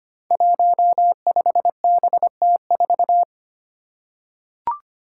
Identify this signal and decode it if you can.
Morse code, heard as a keyed tone, about 25 wpm, 700 Hz. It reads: 15BT4
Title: Five Challenging Letter-Number Combinations 25wpm